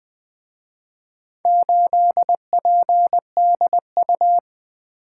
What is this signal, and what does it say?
Morse code, heard as a keyed tone, about 23 wpm, 700 Hz. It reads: 8PDU
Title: Speed-Racing - Four Letter-Number Combinations - 20wpm